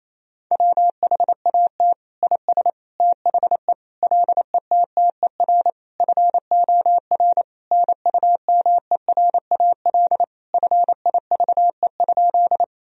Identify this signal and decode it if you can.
Morse code, heard as a keyed tone, about 28 wpm, 700 Hz. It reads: WHAT IS THE LETTER FOR NUMERAL FIVE?
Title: Sentences from Top 1000 Words - Review Part 2 - 28wpm